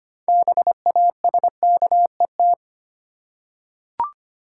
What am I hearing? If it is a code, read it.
Morse code, heard as a keyed tone, about 25 wpm, 700 Hz. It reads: BASKET
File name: practice / 25wpm / Easy Words with Hints - 25wpm